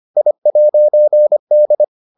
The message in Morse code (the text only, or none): I'D